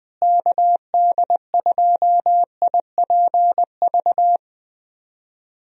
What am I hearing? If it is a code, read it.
Morse code, heard as a keyed tone, about 20 wpm, 700 Hz. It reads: KD2IPV
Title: USA Call Signs - 6 Characters - Repeated 3x - 20wpm